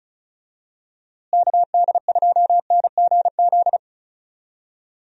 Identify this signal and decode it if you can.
Morse code, heard as a keyed tone, about 34 wpm, 700 Hz. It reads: KD2NGZ
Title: USA Call Signs - 6 Characters - Repeated 2x - 35wpm